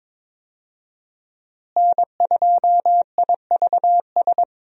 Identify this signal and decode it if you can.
Morse code, heard as a keyed tone, about 22 wpm, 700 Hz. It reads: N2IVS
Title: USA Call Signs - 5 Characters - Repeated 2x - 22wpm